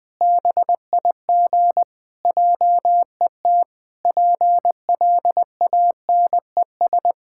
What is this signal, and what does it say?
Morse code, heard as a keyed tone, about 20 wpm, 700 Hz. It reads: BIG JET PLANES
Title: Pangram Challenge v2 - 20wpm